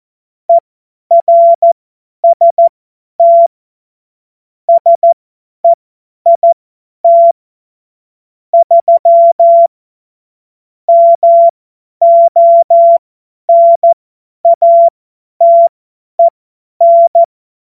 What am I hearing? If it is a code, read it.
Morse code, heard as a keyed tone, about 14 wpm, spaced as for 10 wpm, 680 Hz. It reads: ERST SEIT 3 MONATEN
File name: Morsekurs / Witze / Witz 25. Mai 2021, 50 F70